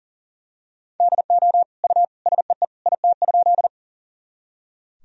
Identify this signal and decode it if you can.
Morse code, heard as a keyed tone, about 40 wpm, 700 Hz. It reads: DO U SEE IT?
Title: Sentences from Top 100 Words and Abbreviations 40wpm